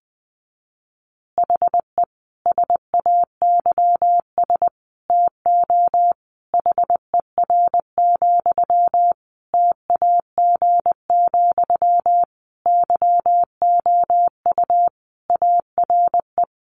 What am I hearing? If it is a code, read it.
Morse code, heard as a keyed tone, about 20 wpm, 700 Hz. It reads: HE SAYS TO HER, TAG, YOU ARE
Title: Additional Easy Phrases 20wpm